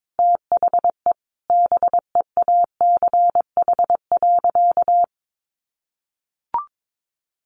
Morse code, heard as a keyed tone, about 22 wpm, 700 Hz. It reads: THE BEACH.
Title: Sentences from Top 1500 Words 22wpm